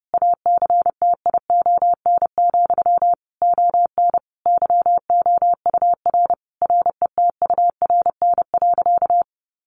Morse code, heard as a keyed tone, about 30 wpm, 700 Hz. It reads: ACTION, ON YOUR RETURN.
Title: A Sound Of Thunder 30wpm